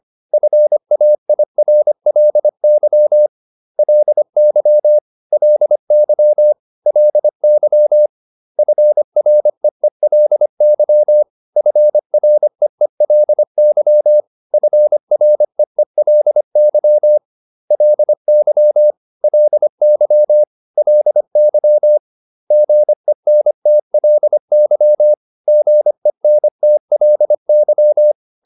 Morse code, heard as a keyed tone, about 25 wpm, 600 Hz. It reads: FAIRLY LY LY LY FREELY FREELY FREELY LY LY LY GENTLY GENTLY